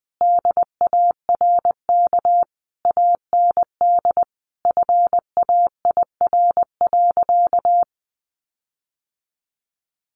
Morse code, heard as a keyed tone, about 20 wpm, 700 Hz. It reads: DARK AND FAIR.
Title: Sentences from Top 1300 Words 20wpm